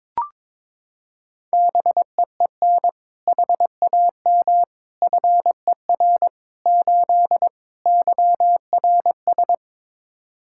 Morse code, heard as a keyed tone, about 22 wpm, 700 Hz. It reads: BEEN HAM FER 8 YRS
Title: Practice QSO - Exchange 3 - More Info - 22wpm